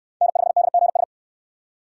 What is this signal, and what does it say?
Morse code, heard as a keyed tone, about 61 wpm, 690 Hz. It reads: N5DGS